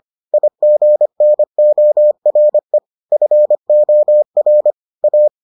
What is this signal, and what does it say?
Morse code, heard as a keyed tone, about 25 wpm, 600 Hz. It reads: IGNORE FOR A